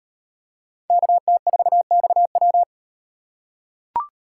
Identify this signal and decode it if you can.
Morse code, heard as a keyed tone, about 38 wpm, 700 Hz. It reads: KT4XW